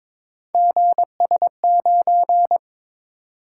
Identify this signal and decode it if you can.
Morse code, heard as a keyed tone, about 22 wpm, 700 Hz. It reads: GS9
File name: practice / 22wpm / Three Letter-Number Combinations - 22wpm